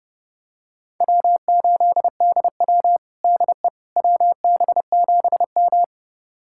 Morse code, heard as a keyed tone, about 30 wpm, 700 Hz. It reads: W8DW DE WB7M
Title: USA Callsign DE Callsign - 4 Characters - 30wpm